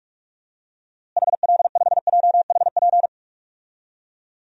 Morse code, heard as a keyed tone, about 41 wpm, 700 Hz. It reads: HL5JHP
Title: DXCC-100 Call Signs - 6 Characters - Repeated 2x - 45wpm